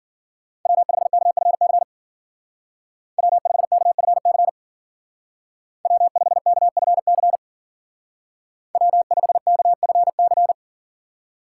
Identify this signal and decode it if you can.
Morse code, heard as a keyed tone, about 52 wpm, 700 Hz. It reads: W5KFC W5KFC W5KFC W5KFC